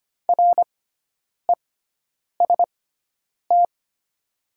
Morse code, heard as a keyed tone, about 25 wpm, spaced as for 9 wpm, 700 Hz. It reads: REST
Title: 6x Character Spacing - Top 500 Words - 25wpm